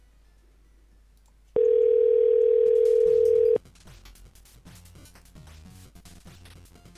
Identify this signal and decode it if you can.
Ringback tone (North America), 440 + 480 Hz.